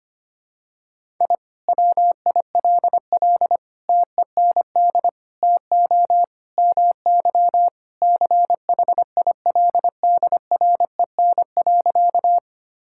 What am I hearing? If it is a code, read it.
Morse code, heard as a keyed tone, about 25 wpm, 700 Hz. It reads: I WILL TEND TO MY CHILDREN.
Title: Sentences from Top 1000 Words - Review Part 5 - 25wpm